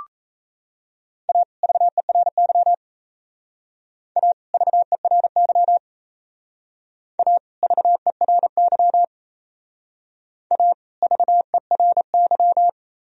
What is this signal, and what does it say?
Morse code, heard as a keyed tone, about 36 wpm, 700 Hz. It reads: A VERY A VERY A VERY A VERY